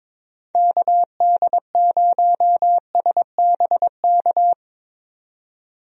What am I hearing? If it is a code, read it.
Morse code, heard as a keyed tone, about 22 wpm, 700 Hz. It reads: KD0SBK
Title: USA Call Signs - 6 Characters - 22wpm